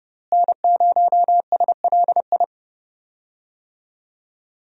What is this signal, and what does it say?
Morse code, heard as a keyed tone, about 30 wpm, 700 Hz. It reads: N0SLI
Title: USA Call Signs - 5 Characters - Repeated 2x - 30wpm